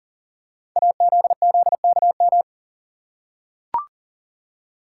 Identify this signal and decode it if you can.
Morse code, heard as a keyed tone, about 39 wpm, 700 Hz. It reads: AZZKM
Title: Code Groups - 5 Letters - 40wpm